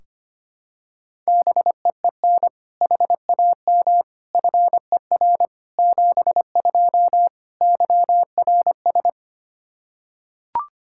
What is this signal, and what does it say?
Morse code, heard as a keyed tone, about 25 wpm, 700 Hz. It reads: BEEN HAM FER 72 YRS
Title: Practice QSO - Exchange 3 - More Info - 25wpm